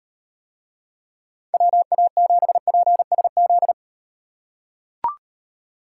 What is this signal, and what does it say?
Morse code, heard as a keyed tone, about 38 wpm, 700 Hz. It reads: WA7PSZ